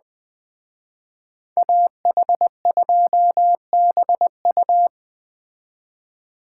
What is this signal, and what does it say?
Morse code, heard as a keyed tone, about 20 wpm, 700 Hz. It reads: AH2BU